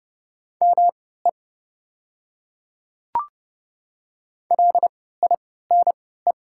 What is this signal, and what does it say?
Morse code, heard as a keyed tone, about 30 wpm, spaced as for 17 wpm, 700 Hz. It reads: ME LINE
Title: 3x Character Spacing - Top 200 Words - 30wpm